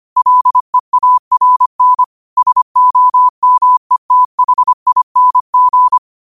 Morse code, heard as a keyed tone, about 25 wpm, 1000 Hz. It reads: LEARN SOMETHING